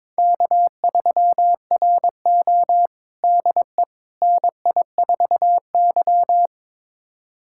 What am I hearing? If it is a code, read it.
Morse code, heard as a keyed tone, about 22 wpm, 700 Hz. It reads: K3RO DE NI4Y
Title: USA Callsign DE Callsign - 4 Characters - 22wpm